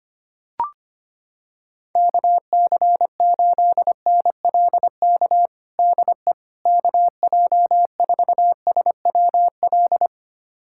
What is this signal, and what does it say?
Morse code, heard as a keyed tone, about 25 wpm, 700 Hz. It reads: KC8NLK DE KJ4SWL
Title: USA Callsign DE Callsign 25wpm